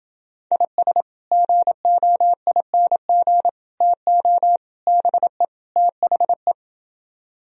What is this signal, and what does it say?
Morse code, heard as a keyed tone, about 28 wpm, 700 Hz. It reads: IS GOING TO BE THE